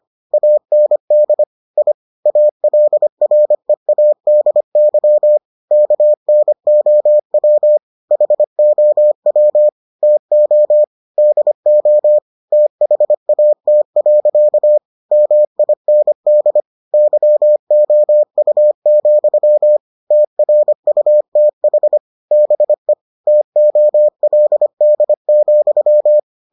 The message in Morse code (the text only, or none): AND I ALREADY KNOW HOW TO DO THAT. MIND YOU, TRUTH BE TOLD,